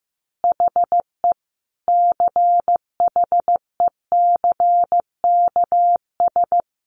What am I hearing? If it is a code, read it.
Morse code, heard as a keyed tone, about 15 wpm, 700 Hz. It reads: HE CHECKS
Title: Easy Phrases 15wpm